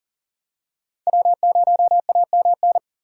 Morse code, heard as a keyed tone, about 40 wpm, 700 Hz. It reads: W0AMN
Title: USA Call Signs - 5 Characters - 8x Character Spacing - 40wpm